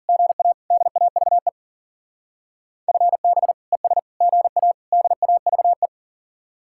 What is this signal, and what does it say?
Morse code, heard as a keyed tone, about 50 wpm, 700 Hz. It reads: GA DAVE FB ES GA DAVE
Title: Speed-Racing Common QSO Phrases - 40wpm